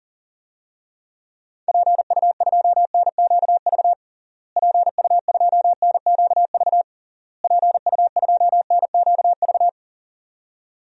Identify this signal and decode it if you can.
Morse code, heard as a keyed tone, about 40 wpm, 700 Hz. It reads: PU2NQV PU2NQV PU2NQV
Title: DXCC-100 Call Signs - 6 Characters - Repeated 3x - 40wpm